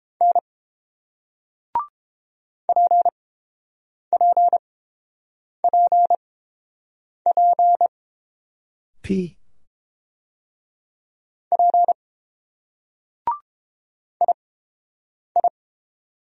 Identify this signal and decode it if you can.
Morse code, heard as a keyed tone, about 30 wpm, spaced as for 8 wpm, 700 Hz. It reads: N PPPP P II